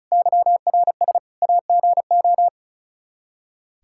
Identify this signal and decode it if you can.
Morse code, heard as a keyed tone, about 34 wpm, 700 Hz. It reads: YRS AGO
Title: K1USN SST QSO Practice 35wpm